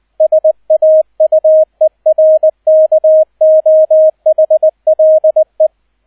Morse code, heard as a keyed tone, about 20 wpm, 630 Hz, with radio noise. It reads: SAUERKOHLE
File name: 002 Zweiter Streich 20wpm